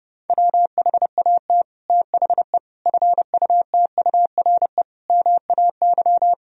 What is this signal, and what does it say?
Morse code, heard as a keyed tone, about 30 wpm, 700 Hz. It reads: WHAT THE FUTURE MAY
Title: Sentences from Top 1000 Words - Review Part 2 - 30wpm